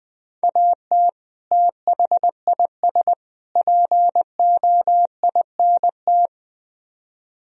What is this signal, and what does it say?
Morse code, heard as a keyed tone, about 20 wpm, 700 Hz. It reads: AT THIS POINT
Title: Speed-Racing - Sets of 3 Words 17wpm